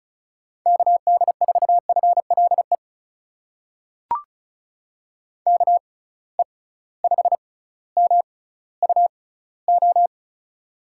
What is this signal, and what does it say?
Morse code, heard as a keyed tone, about 34 wpm, spaced as for 18 wpm, 700 Hz. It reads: KD4FLE KE5MUO
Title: USA Call Signs - 6 Characters - 6x Character Spacing - 35wpm